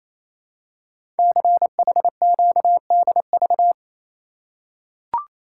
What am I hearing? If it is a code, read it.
Morse code, heard as a keyed tone, about 27 wpm, 700 Hz. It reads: CHQDV
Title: Code Groups - 5 Letter:Numbers - 28wpm